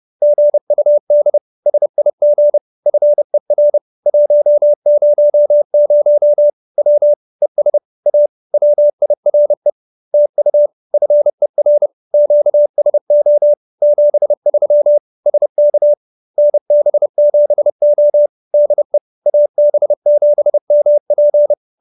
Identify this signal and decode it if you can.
Morse code, heard as a keyed tone, about 30 wpm, 590 Hz. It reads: GUD SIG FER 100 W ES A WIRE TU FER QSO 73 SK NB7O DE AB7MP